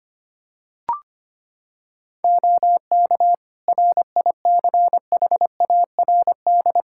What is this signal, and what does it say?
Morse code, heard as a keyed tone, about 25 wpm, 700 Hz. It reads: OK RICHARD
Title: Practice QSO - Exchange 3 - More Info - 25wpm